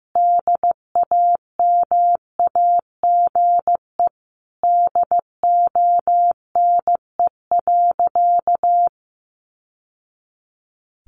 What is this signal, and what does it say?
Morse code, heard as a keyed tone, about 15 wpm, 700 Hz. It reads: DAMAGE DONE.
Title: Sentences from Top 1100 Words 15wpm